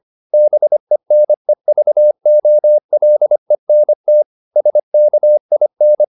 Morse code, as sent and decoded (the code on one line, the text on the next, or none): -... -. ...- --- .-.. -. - / ... -.- .. -.
BENEVOLENT SKIN